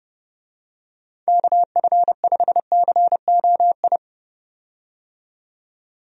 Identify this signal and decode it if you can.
Morse code, heard as a keyed tone, about 31 wpm, 700 Hz. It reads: KF5COI
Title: USA Call Signs - 6 Characters - 30wpm